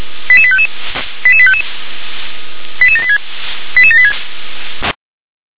Five-tone selective call (ZVEI), heard as quick runs of five tones, tones of about 70 ms. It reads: 88855 88855 88786 88786